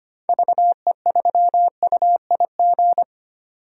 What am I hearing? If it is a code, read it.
Morse code, heard as a keyed tone, about 25 wpm, 700 Hz. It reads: VE3UIG
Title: DXCC-100 Call Signs - 6 Characters - 25wpm